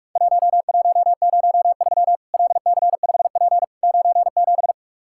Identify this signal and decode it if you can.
Morse code, heard as a keyed tone, about 44 wpm, 700 Hz. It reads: 1103 LC5P 97
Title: Worked All Europe DX Contest - QTC Exchange - 45wpm